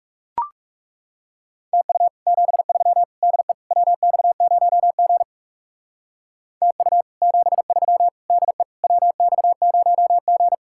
Morse code, heard as a keyed tone, about 39 wpm, 700 Hz. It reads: TU 73 DE WX0G TU 73 DE WX0G